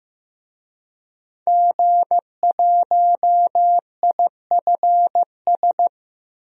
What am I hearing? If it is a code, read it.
Morse code, heard as a keyed tone, about 15 wpm, 700 Hz. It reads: G1IFS
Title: DXCC-100 Call Signs - 5 Characters - Repeated 3x - 15wpm